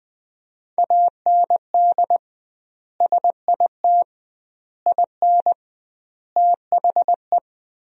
Morse code, heard as a keyed tone, about 20 wpm, 700 Hz. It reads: AND SIT IN THE
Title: 2x Word-Spacing - Sentences from Top 500 Words - 20wpm